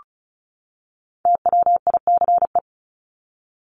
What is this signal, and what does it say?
Morse code, heard as a keyed tone, about 34 wpm, 700 Hz. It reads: TWICE
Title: Top 1501-1750 Words 35wpm